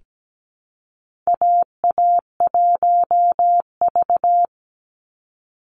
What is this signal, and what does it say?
Morse code, heard as a keyed tone, about 17 wpm, 700 Hz. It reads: AA1V
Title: Call Signs Repeated 2x - 17wpm